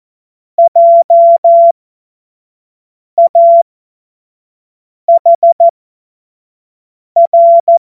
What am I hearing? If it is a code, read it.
Morse code, heard as a keyed tone, about 14 wpm, spaced as for 5 wpm, 680 Hz. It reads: JAHR
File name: Morsekurs / Witze / Witz 20. Dezember 2022 25 F70